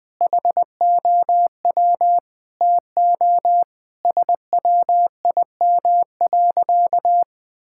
Morse code, heard as a keyed tone, about 20 wpm, 700 Hz. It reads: HOW TO SWIM.
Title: Sentences with 3-Word Phrases - 20wpm